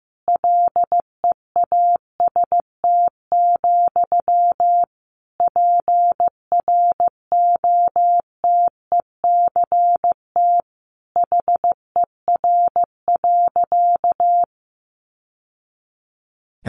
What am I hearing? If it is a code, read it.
Morse code, heard as a keyed tone, about 15 wpm, 700 Hz. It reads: LEAST, PROTECT HER.
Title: Sentences from Top 900 Words 15wpm